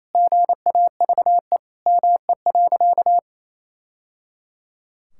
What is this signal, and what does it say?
Morse code, heard as a keyed tone, about 27 wpm, 700 Hz. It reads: GAVE ME.
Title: Sentences from Top 1000 Words - Review Part 6 - 28wpm